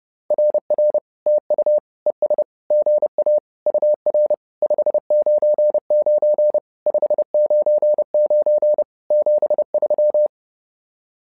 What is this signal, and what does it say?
Morse code, heard as a keyed tone, about 30 wpm, 600 Hz. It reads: RR TU ES GA UR 599 599 73
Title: SOTA QSO Practice 30wpm